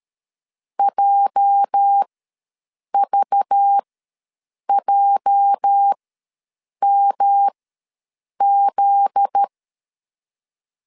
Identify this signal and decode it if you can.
Morse code, heard as a keyed tone, about 13 wpm, spaced as for 7 wpm, 800 Hz. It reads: JVJMZ
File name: LEZIONI / 8 YJ45 / 8 YJ ws 12wpm 3342